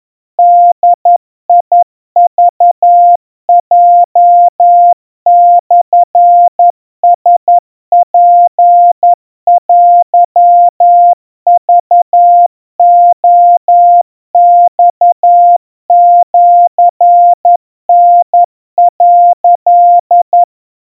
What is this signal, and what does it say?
Morse code (Japanese Wabun code), heard as a keyed tone, about 11 wpm, 700 Hz. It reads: ボクヲモラツテクレマシタ」
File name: kawa / Morse sample / kamyu 40 cpm 700 Hz Crispy